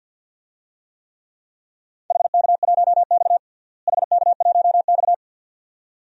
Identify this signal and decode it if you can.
Morse code, heard as a keyed tone, about 52 wpm, 700 Hz. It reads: HK1X HK1X
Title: DXCC-100 Call Signs - 4 Characters - Repeated 2x - 50wpm